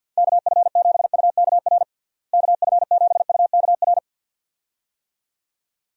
Morse code, heard as a keyed tone, about 50 wpm, 700 Hz. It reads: KF7UKR KF7UKR
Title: USA Call Signs - 6 Characters - Repeated 2x - 50wpm